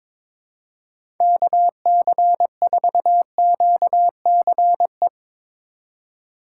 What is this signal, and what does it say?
Morse code, heard as a keyed tone, about 21 wpm, 700 Hz. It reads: KC4QCE
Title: USA Call Signs - 6 Characters - 2x Character Spacing - 22wpm